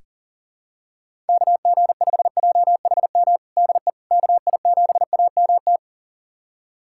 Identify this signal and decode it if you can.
Morse code, heard as a keyed tone, about 37 wpm, 700 Hz. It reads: KG5JHM DE KI7AMT
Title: USA Callsign DE Callsign - 6 Characters - 40wpm